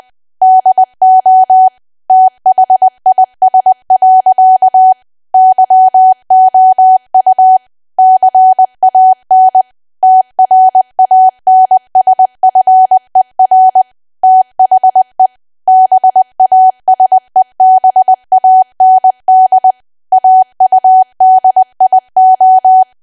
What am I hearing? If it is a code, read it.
Morse code, heard as a keyed tone, about 20 wpm, 750 Hz. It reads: DO THIS. YOU CAN TRANSFER THE BASEBAND AUDIO
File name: cwa / W1AW-Level3 / 20WPM